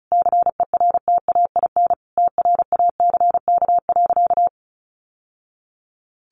Morse code, heard as a keyed tone, about 34 wpm, 700 Hz. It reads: CERTAIN TRACK.
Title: Sentences from Top 1100 Words 35wpm